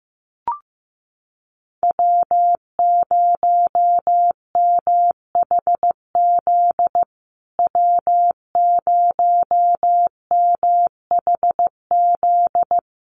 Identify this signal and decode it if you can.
Morse code, heard as a keyed tone, about 15 wpm, 700 Hz. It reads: W0MHZ W0MHZ
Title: USA Call Signs Repeated 3x 15wpm